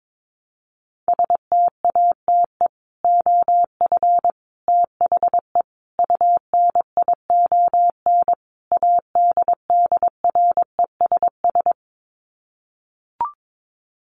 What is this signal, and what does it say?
Morse code, heard as a keyed tone, about 22 wpm, 700 Hz. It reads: STATE OF THE UNION ADDRESS
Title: Sets of 5 Words Encore 22wpm